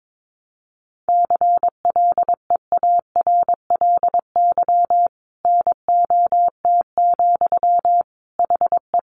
Morse code, heard as a keyed tone, about 22 wpm, 700 Hz. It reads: CLEARLY NOT, HE
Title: Sentences from Top 1300 Words 22wpm